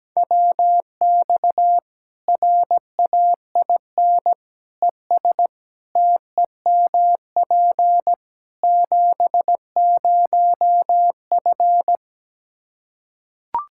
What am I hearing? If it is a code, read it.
Morse code, heard as a keyed tone, about 17 wpm, 700 Hz. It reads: WX RAIN ES TEMP 70F
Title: Practice QSO - Exchange 2 - Info - 17wpm